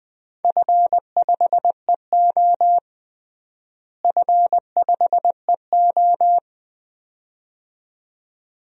Morse code, heard as a keyed tone, about 21 wpm, 700 Hz. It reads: F5EO F5EO
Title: DXCC-100 Call Signs - 4 Characters - Repeated 2x - 20wpm